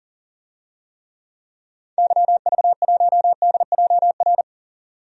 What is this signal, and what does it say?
Morse code, heard as a keyed tone, about 40 wpm, 700 Hz. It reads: YV1DJR
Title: DXCC-100 Call Signs - 6 Characters - 40wpm